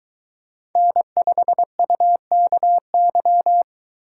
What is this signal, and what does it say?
Morse code, heard as a keyed tone, about 20 wpm, 700 Hz. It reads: N5UKY